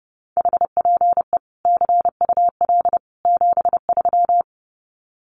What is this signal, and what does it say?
Morse code, heard as a keyed tone, about 30 wpm, 700 Hz. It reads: HPE CUL 73
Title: Common QSO Phrases 30wpm